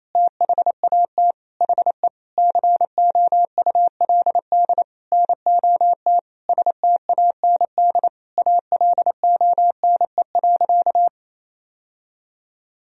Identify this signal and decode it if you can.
Morse code, heard as a keyed tone, about 27 wpm, 700 Hz. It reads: THAT HE COULD NOT STAND ALONE.
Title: Sentences from Top 1000 Words - Review Part 3 - 28wpm